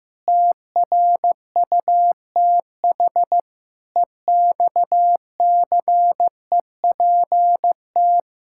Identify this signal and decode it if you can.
Morse code, heard as a keyed tone, about 15 wpm, 700 Hz. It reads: TRUTH EXCEPT
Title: Sentences from Top 1600 Words 15wpm